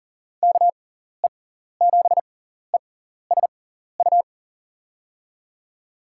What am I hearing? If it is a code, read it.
Morse code, heard as a keyed tone, about 39 wpm, spaced as for 14 wpm, 700 Hz. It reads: KE7ESU
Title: USA Call Signs - 6 Characters - 6x Character Spacing - 40wpm